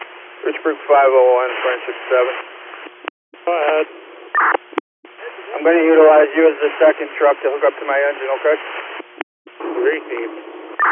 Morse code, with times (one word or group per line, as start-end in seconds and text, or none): none